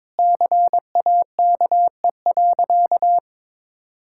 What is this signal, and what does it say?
Morse code, heard as a keyed tone, about 22 wpm, 700 Hz. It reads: CAKE.
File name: practice / 22wpm / Sentences with 2-Word Phrases - 22wpm